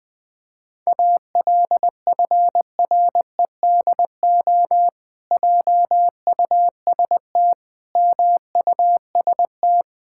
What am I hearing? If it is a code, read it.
Morse code, heard as a keyed tone, about 20 wpm, 700 Hz. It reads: ALFREDO JUST MUST
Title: Pangram Challenge v2 - 20wpm